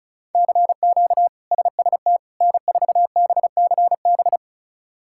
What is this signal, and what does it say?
Morse code, heard as a keyed tone, about 34 wpm, 700 Hz. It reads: CQ SST N4BCB
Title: K1USN SST QSO Practice - 35wpm